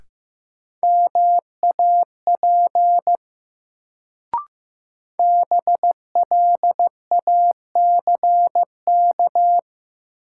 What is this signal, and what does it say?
Morse code, heard as a keyed tone, about 15 wpm, 700 Hz. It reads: MAP BLACK
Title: Sentences from Top 400 Words - New Word Drill - 15wpm